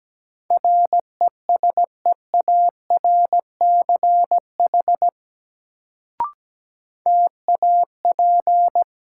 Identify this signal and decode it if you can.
Morse code, heard as a keyed tone, about 17 wpm, 700 Hz. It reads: RESEARCH TAP
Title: Sentences from Top 1400 Words - New Word Drill Alt - 17wpm